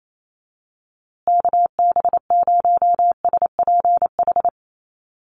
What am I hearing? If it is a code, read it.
Morse code, heard as a keyed tone, about 28 wpm, 700 Hz. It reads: KB0SPH
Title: USA Call Signs Repeated 2x 28wpm